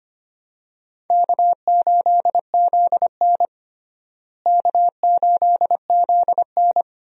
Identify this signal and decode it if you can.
Morse code, heard as a keyed tone, about 25 wpm, 700 Hz. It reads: K8ZN K8ZN
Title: USA Call Signs - 4 Characters - Repeated 3x - 25wpm